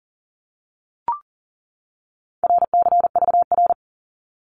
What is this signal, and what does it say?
Morse code, heard as a keyed tone, about 40 wpm, 700 Hz. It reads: RCVR
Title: Abbreviations 40wpm